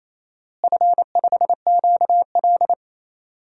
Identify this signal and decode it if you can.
Morse code, heard as a keyed tone, about 28 wpm, 700 Hz. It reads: F5QL
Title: DXCC-100 Call Signs - 4 Characters - Repeated 3x - 28wpm